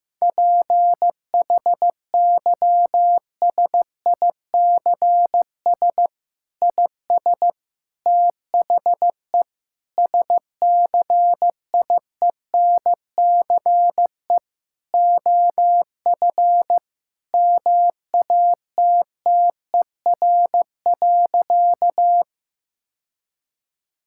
Morse code, heard as a keyed tone, about 16 wpm, 700 Hz. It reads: PHYSICS IS THE SCIENCE OF MATTER.
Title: Sentences from Top 1000 Words - Review Part 5 - 15wpm